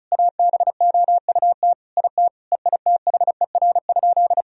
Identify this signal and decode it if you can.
Morse code, heard as a keyed tone, about 34 wpm, 700 Hz. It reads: ABOUT IT EITHER?
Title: Sentences from Top 1000 Words - Review Part 4 - 35wpm